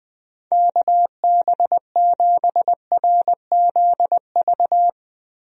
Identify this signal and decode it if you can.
Morse code, heard as a keyed tone, about 21 wpm, 700 Hz. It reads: KB7RZV
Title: USA Call Signs - 6 Characters - Repeated 2x - 20wpm